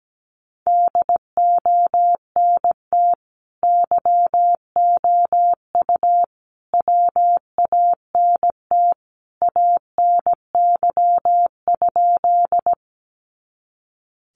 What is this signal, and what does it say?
Morse code, heard as a keyed tone, about 17 wpm, 700 Hz. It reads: DONT YOU WANT ANY?
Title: Sentences from Top 300 Words 17wpm